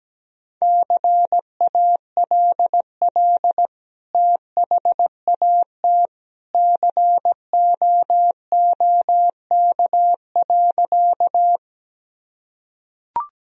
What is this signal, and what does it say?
Morse code, heard as a keyed tone, about 17 wpm, 700 Hz. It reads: CALL THAT COOK.
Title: Sentences from Top 1000 Words - Review Part 4 - 17wpm